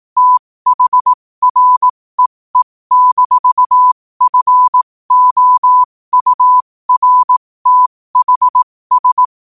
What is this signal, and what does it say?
Morse code, heard as a keyed tone, about 18 wpm, spaced as for 15 wpm, 1000 Hz. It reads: THREE-FOURTHS
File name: qotdfiles / 15wpm